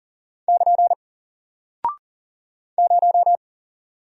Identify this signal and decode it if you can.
Morse code, heard as a keyed tone, about 40 wpm, 700 Hz. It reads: <KN> 0
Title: Single Character (Letter:Number:Punctuation:Prosign) Alt 40wpm